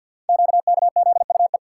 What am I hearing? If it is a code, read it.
Morse code, heard as a keyed tone, about 50 wpm, 690 Hz. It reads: QKZUE
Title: Code Groups - 5 Letters - 50wpm